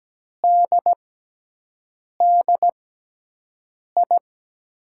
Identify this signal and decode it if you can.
Morse code, heard as a keyed tone, about 17 wpm, spaced as for 6 wpm, 700 Hz. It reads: DDI
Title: USA Call Signs - 6 Characters - 6x Character Spacing - 17wpm